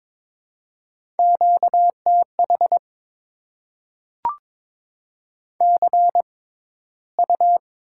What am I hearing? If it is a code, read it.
Morse code, heard as a keyed tone, about 22 wpm, spaced as for 14 wpm, 700 Hz. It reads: QTH CU
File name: Top 20 QSO Elements - 6x Character Spacing - 22wpm